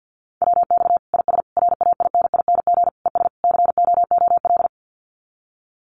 Morse code, heard as a keyed tone, about 50 wpm, 700 Hz. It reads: WX IS RAINING ES COOL
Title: Common QSO Phrases 50wpm